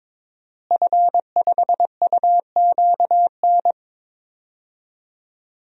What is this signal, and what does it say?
Morse code, heard as a keyed tone, about 22 wpm, 700 Hz. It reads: F5UQN
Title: DXCC-100 Call Signs - 5 Characters - Repeated 3x - 22wpm